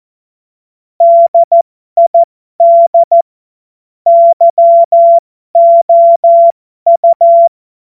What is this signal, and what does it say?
Morse code, heard as a keyed tone, about 14 wpm, spaced as for 12 wpm, 680 Hz. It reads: DID YOU
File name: Morsekurs / Witze